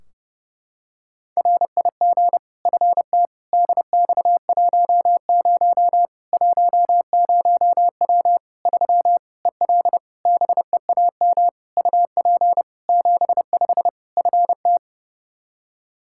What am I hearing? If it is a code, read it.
Morse code, heard as a keyed tone, about 30 wpm, 700 Hz. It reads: RIG FT DX10 10W 3 EL BEAM UP 75 FT
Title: K1USN SST QSO Practice 30wpm